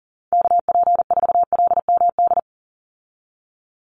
Morse code, heard as a keyed tone, about 38 wpm, 700 Hz. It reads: KP4LMD
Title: USA Call Signs 40wpm